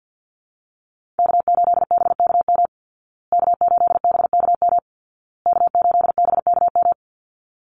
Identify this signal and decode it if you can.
Morse code, heard as a keyed tone, about 51 wpm, 700 Hz. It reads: K8BKM K8BKM K8BKM
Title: Call Signs Repeated 2x - 50wpm